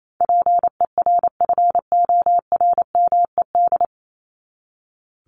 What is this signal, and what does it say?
Morse code, heard as a keyed tone, about 28 wpm, 700 Hz. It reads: PERFORMED
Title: ED Suffix - Top 100 Words with Base Words 28wpm